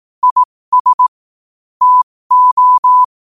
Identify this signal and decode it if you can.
Morse code, heard as a keyed tone, about 18 wpm, spaced as for 15 wpm, 1000 Hz. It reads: IS TO
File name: qotdfiles / 15wpm